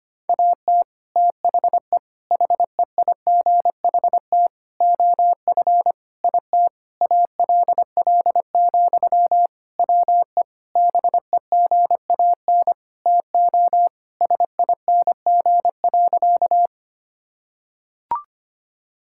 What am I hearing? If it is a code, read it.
Morse code, heard as a keyed tone, about 25 wpm, 700 Hz. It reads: AT THE HEIGHT OF IT ALL, WE BEGAN TO SING.
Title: Sentences from Top 1600 Words 25wpm